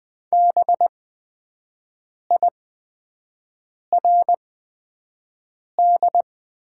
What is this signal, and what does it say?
Morse code, heard as a keyed tone, about 20 wpm, spaced as for 5 wpm, 700 Hz. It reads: BIRD
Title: 8x Character Spacing - Top 500 Words - 20wpm